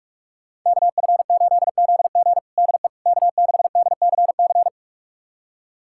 Morse code, heard as a keyed tone, about 46 wpm, 700 Hz. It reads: KF8ZG DE K6DCC
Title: USA Callsign DE Callsign - 5 Characters - 45wpm